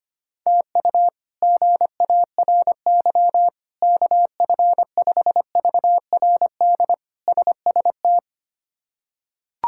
Morse code, heard as a keyed tone, about 25 wpm, 700 Hz. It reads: TU GARY KF5VRD SST
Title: K1USN SST QSO Practice - 25wpm